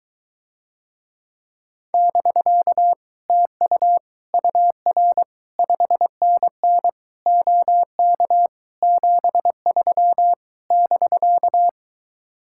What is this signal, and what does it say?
Morse code, heard as a keyed tone, about 24 wpm, 700 Hz. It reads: <BK> TU UR 5NN OK 73 <BK>